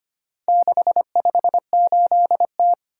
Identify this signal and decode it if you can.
Morse code, heard as a keyed tone, about 25 wpm, 700 Hz. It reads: 658T
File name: practice / 25wpm / Four Letter-Number Combinations - 25wpm